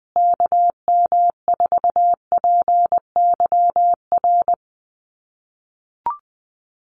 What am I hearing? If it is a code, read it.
Morse code, heard as a keyed tone, about 20 wpm, 700 Hz. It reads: KM4PYR
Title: USA Call Signs Repeated 2x 20wpm